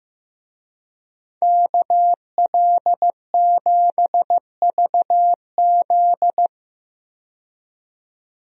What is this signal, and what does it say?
Morse code, heard as a keyed tone, about 16 wpm, 700 Hz. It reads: KL7VZ
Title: DXCC-100 Call Signs - 5 Characters - 15wpm